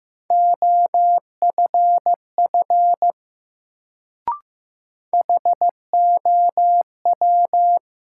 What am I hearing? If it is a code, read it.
Morse code, heard as a keyed tone, about 15 wpm, 700 Hz. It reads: OFF HOW
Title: Sets of 2 Words - New Word Drill - 15wpm